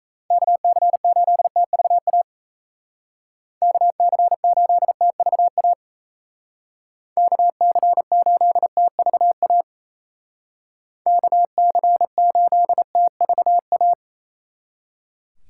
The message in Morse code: KC8TVA KC8TVA KC8TVA KC8TVA

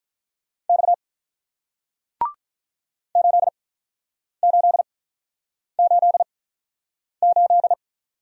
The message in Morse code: X 8888

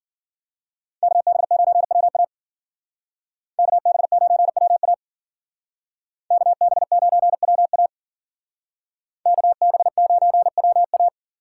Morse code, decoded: KB9WA KB9WA KB9WA KB9WA

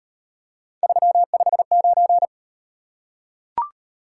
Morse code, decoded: 359

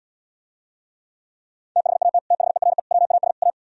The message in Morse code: THAT THERE WAS A

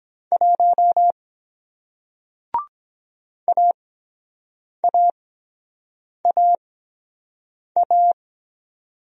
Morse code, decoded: 1 AAAA

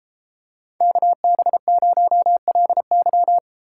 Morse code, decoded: KB0LY

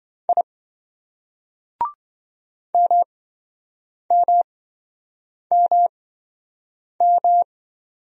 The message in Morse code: I MMMM

